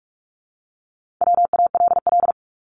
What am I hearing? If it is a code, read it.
Morse code, heard as a keyed tone, about 46 wpm, 700 Hz. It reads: WALL
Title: Multi-letter Phonemes 45wpm